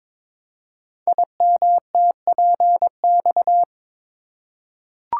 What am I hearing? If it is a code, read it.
Morse code, heard as a keyed tone, about 22 wpm, 700 Hz. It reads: IMTPX
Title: Code Groups - 5 Letters - 22wpm